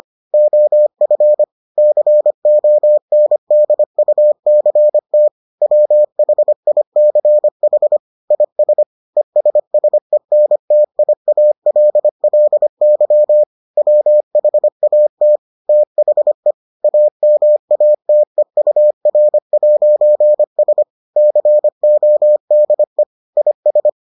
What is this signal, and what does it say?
Morse code, heard as a keyed tone, about 25 wpm, 600 Hz. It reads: OF CONDUCT WHICH IS ESSENTIALLY WHAT THE AMATEUR'S CODE IS